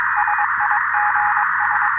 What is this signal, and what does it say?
Morse code, heard as a keyed tone, about 22 wpm, 840 Hz, with radio noise. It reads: SIGS
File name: attachments / 13.03.2015 RV3GM QSO UA1CEG P QRPP